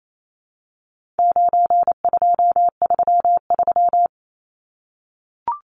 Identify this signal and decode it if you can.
Morse code, heard as a keyed tone, about 28 wpm, 700 Hz. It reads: 9233
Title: Numbers - 4-Digit 28wpm